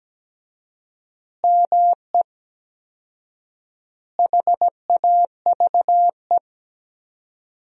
Morse code, heard as a keyed tone, about 17 wpm, 700 Hz. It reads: ME HAVE
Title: 4x Word-Spacing - Sentences from Top 300 Words - 17wpm